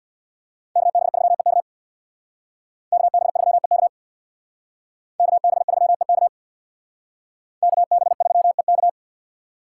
Morse code, heard as a keyed tone, about 62 wpm, 700 Hz. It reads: KB3EK KB3EK KB3EK KB3EK
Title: Speed Racing - USA 2x2 Call Signs 50wpm